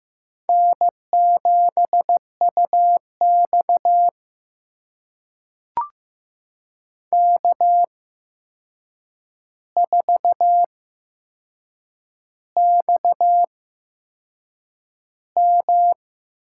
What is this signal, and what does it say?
Morse code, heard as a keyed tone, about 15 wpm, spaced as for 6 wpm, 700 Hz. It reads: N7UX K4XM
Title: USA Call Signs - 4 Characters - 8x Character Spacing - 15wpm